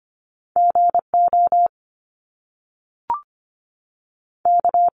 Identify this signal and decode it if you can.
Morse code, heard as a keyed tone, about 25 wpm, 700 Hz. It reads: GO K
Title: Top Two Letter Combinations in Words 25wpm